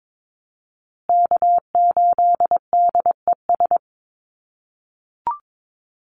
Morse code, heard as a keyed tone, about 22 wpm, 700 Hz. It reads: K8DES
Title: USA Call Signs Repeated 3x 22wpm